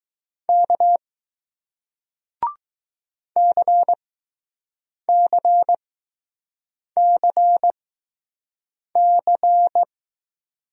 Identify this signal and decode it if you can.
Morse code, heard as a keyed tone, about 20 wpm, spaced as for 6 wpm, 700 Hz. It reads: K CCCC